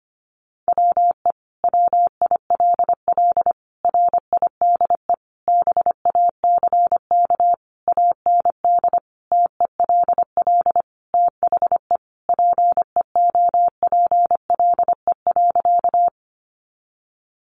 Morse code, heard as a keyed tone, about 25 wpm, 700 Hz. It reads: WE WILL RIDE BACK AND TELL THE PEOPLE.
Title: Sentences from Top 1200 Words 25wpm